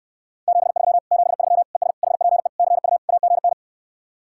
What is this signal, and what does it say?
Morse code, heard as a keyed tone, about 69 wpm, 700 Hz. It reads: 73 73 ES HPE CU AGN